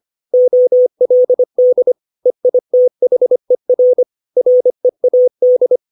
Morse code, heard as a keyed tone, about 25 wpm, 500 Hz. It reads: OLD EITHER READ
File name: 632 words / 632 Words 25WPM